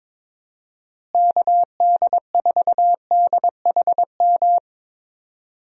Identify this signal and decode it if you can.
Morse code, heard as a keyed tone, about 22 wpm, 700 Hz. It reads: KD4DHM
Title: USA Call Signs - 6 Characters - Repeated 2x - 22wpm